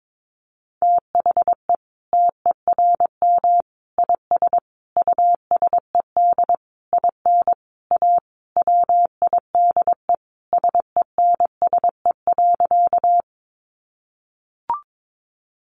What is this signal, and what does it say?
Morse code, heard as a keyed tone, about 22 wpm, 700 Hz. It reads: THE TERM IS USED IN A WIDE SENSE.
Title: Sentences from Top 900 Words 22wpm